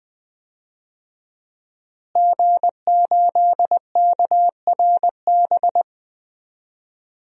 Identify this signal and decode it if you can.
Morse code, heard as a keyed tone, about 20 wpm, 700 Hz. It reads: G8KRB
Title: DXCC-100 Call Signs - 5 Characters - Repeated 2x - 20wpm